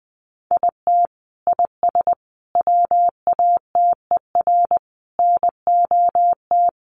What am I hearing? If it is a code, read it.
Morse code, heard as a keyed tone, about 20 wpm, 700 Hz. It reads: IT IS WATER NOT
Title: Sentences from Top 1100 Words 20wpm